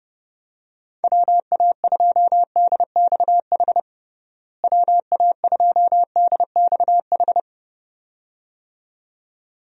Morse code, heard as a keyed tone, about 32 wpm, 700 Hz. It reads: WA2DXH WA2DXH
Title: USA Call Signs - 6 Characters - Repeated 2x - 30wpm